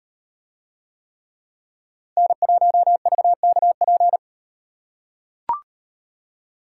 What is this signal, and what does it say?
Morse code, heard as a keyed tone, about 38 wpm, 700 Hz. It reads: N1VKP